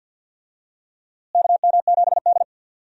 Morse code, heard as a keyed tone, about 48 wpm, 700 Hz. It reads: KM7D
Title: USA Call Signs - 4 Characters - Repeated 2x - 50wpm